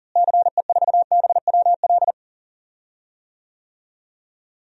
Morse code, heard as a keyed tone, about 40 wpm, 700 Hz. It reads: CE4BWL
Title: DXCC-100 Call Signs - 6 Characters - Repeated 2x - 40wpm